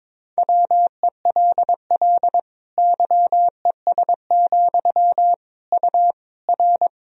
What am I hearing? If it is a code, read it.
Morse code, heard as a keyed tone, about 22 wpm, 700 Hz. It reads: WELL YES, U R
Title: Sentences from Top 500 Words and Abbreviations 22wpm